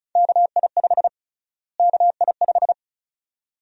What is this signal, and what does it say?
Morse code, heard as a keyed tone, about 36 wpm, 700 Hz. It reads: KI5 KI5
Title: USA Call Sign Prefixes - Repeated 2x - 35wpm